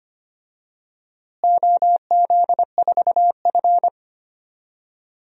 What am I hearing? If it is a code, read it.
Morse code, heard as a keyed tone, about 25 wpm, 700 Hz. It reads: OZ4F
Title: DXCC-100 Call Signs - 4 Characters - Repeated 3x - 25wpm